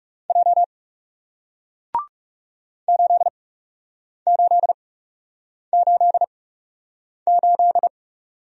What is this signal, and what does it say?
Morse code, heard as a keyed tone, about 40 wpm, spaced as for 8 wpm, 700 Hz. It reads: J 8888